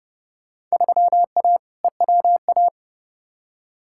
Morse code, heard as a keyed tone, about 32 wpm, 700 Hz. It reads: 3A EWA
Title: Field Day Exchanges - 30wpm